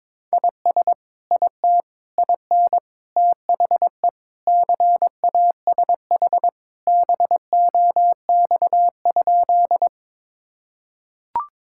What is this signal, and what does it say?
Morse code, heard as a keyed tone, about 22 wpm, 700 Hz. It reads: IS IT IN THE CASH BOX?
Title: Sentences from Top 1000 Words - Review Part 5 - 22wpm